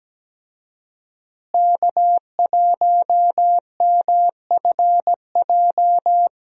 Morse code, heard as a keyed tone, about 17 wpm, 700 Hz. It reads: K1MFJ
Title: USA Call Signs - 5 Characters - Repeated 3x - 17wpm